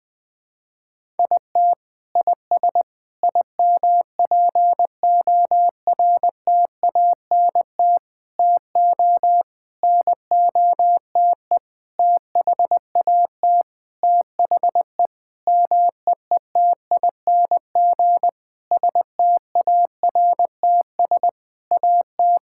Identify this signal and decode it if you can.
Morse code, heard as a keyed tone, about 20 wpm, 700 Hz. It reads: IT IS IMPORTANT TO NOTE THAT THE MEETING STARTS AT
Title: Sentences with 7-Word Phrases - 20wpm